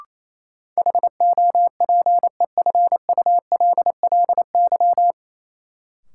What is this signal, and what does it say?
Morse code, heard as a keyed tone, about 28 wpm, 700 Hz. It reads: HOPEFULLY
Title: Sentences from Top 1500 Words - New Word Drill - 28wpm